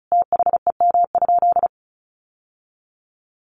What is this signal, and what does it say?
Morse code, heard as a keyed tone, about 34 wpm, 700 Hz. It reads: THEM?
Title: Sentences from Top 600 Words 35wpm